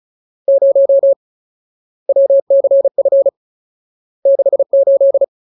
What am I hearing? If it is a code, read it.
Morse code, heard as a keyed tone, about 34 wpm, 550 Hz. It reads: 0 WCF 68
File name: Ultimate Word List / 35WPM Ultimate